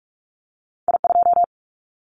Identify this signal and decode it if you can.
Morse code, heard as a keyed tone, about 46 wpm, 700 Hz. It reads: I2
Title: Callsign Prefixes for Top 25 DXCC - 45wpm